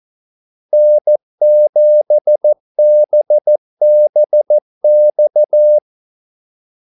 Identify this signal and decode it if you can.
Morse code, heard as a keyed tone, about 14 wpm, 600 Hz. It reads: N7BBX
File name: Callsigns-14wpm